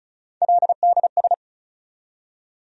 Morse code, heard as a keyed tone, about 34 wpm, 700 Hz. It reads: LDS
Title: USA Call Sign Postfixes - 3 Letters 35wpm